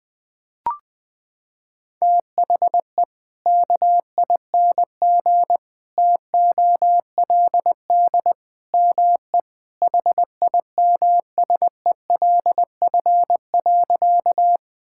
Morse code, heard as a keyed tone, about 20 wpm, 700 Hz. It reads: THE KING TOLD ME HIMSELF.
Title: Sentences from Top 1600 Words 20wpm